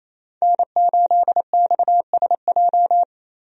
Morse code, heard as a keyed tone, about 28 wpm, 700 Hz. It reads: N8XSJ
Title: USA Call Signs - 5 Characters - Repeated 3x - 28wpm